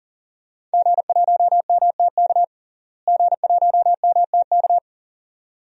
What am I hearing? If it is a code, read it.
Morse code, heard as a keyed tone, about 40 wpm, 700 Hz. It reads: G1MTK G1MTK
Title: DXCC-100 Call Signs - 5 Characters - Repeated 2x - 40wpm